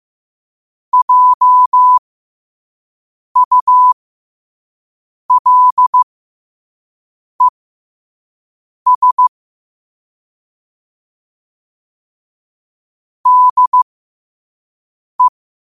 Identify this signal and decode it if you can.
Morse code, heard as a keyed tone, about 15 wpm, spaced as for 5 wpm, 1000 Hz. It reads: JULES DE
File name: qotdfiles / 5wpm